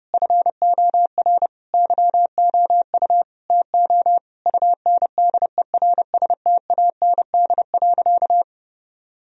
Morse code, heard as a keyed tone, about 30 wpm, 700 Hz. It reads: FOR YOU TO UNDERSTAND.
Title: Sentences from Top 1000 Words - Review Part 3 - 30wpm